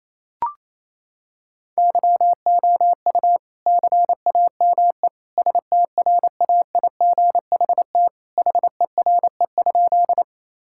Morse code, heard as a keyed tone, about 28 wpm, 700 Hz. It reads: YOU CAME STRAIGHT HERE?
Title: Sentences from Top 1000 Words - Review Part 6 - 28wpm